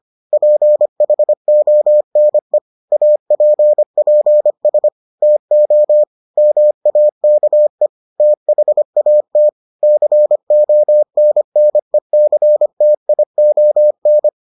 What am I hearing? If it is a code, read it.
Morse code, heard as a keyed tone, about 25 wpm, 600 Hz. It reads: PHONE APPS TO MAKE THAT CONNECTION